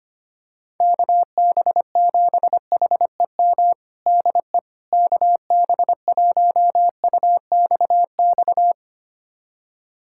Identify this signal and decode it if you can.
Morse code, heard as a keyed tone, about 25 wpm, 700 Hz. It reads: KB7HEM DE KB1UXX
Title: USA Callsign DE Callsign - 6 Characters - 25wpm